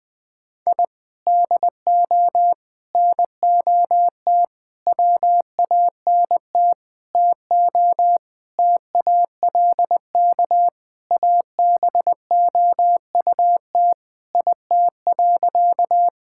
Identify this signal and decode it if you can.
Morse code, heard as a keyed tone, about 20 wpm, 700 Hz. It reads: I DO NOT WANT TO TALK ABOUT IT.
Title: Sentences with 5-Word Phrases - 20wpm